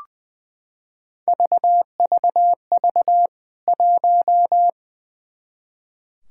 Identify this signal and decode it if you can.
Morse code, heard as a keyed tone, about 20 wpm, 700 Hz. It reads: VVV 1
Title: K1USN SST QSO Practice 20wpm